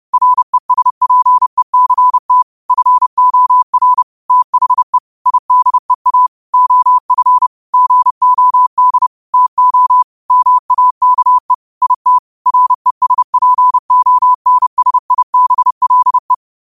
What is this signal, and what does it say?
Morse code, heard as a keyed tone, about 30 wpm, 1000 Hz. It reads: RESPECT FOR THE IDEA OF GOD TO MAKE IT RESPONSIBLE
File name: qotdfiles / 30wpm